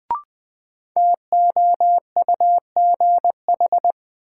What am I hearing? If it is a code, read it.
Morse code, heard as a keyed tone, about 20 wpm, 700 Hz. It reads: TOUGH
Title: Sentences from Top 1200 Words - New Word Drill - 20wpm